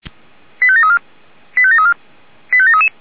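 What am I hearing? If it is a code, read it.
Five-tone selective call (ZVEI), heard as quick runs of five tones, tones of about 70 ms. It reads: 85734 85734 85733
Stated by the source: ZVEI five-tone selective call received on SDR.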